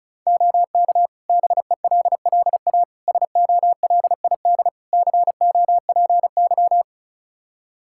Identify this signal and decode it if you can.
Morse code, heard as a keyed tone, about 36 wpm, 700 Hz. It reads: OK BELLA SOLID COPY
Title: K1USN SST QSO Practice 35wpm